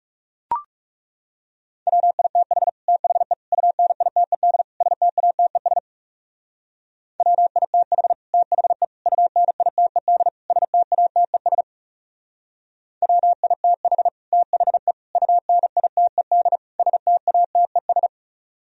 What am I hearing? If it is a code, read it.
Morse code, heard as a keyed tone, about 38 wpm, 700 Hz. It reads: WITH THE UNITED STATES WITH THE UNITED STATES WITH THE UNITED STATES